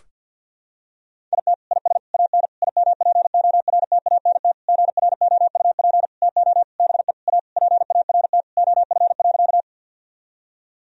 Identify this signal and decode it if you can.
Morse code, heard as a keyed tone, about 50 wpm, 700 Hz. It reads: IT IS AN IMPORTANT GROUP TO BE A PART OF.
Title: Sentences from Top 1000 Words - Review Part 1 - 50wpm